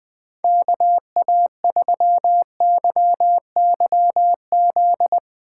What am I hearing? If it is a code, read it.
Morse code, heard as a keyed tone, about 20 wpm, 700 Hz. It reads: KA3YYZ
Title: USA Call Signs - 6 Characters - 8x Character Spacing - 20wpm